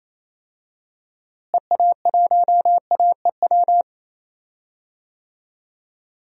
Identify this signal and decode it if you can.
Morse code, heard as a keyed tone, about 28 wpm, 700 Hz. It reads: EA1AEW
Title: DXCC-100 Call Signs - 6 Characters - Repeated 3x - 28wpm